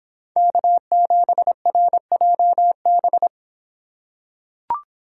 Morse code, heard as a keyed tone, about 25 wpm, 700 Hz. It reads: K7RJB